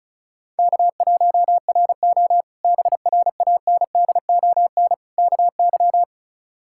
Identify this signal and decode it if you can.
Morse code, heard as a keyed tone, about 36 wpm, 700 Hz. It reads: K1RO BRANDON KY
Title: K1USN SST QSO Practice - 35wpm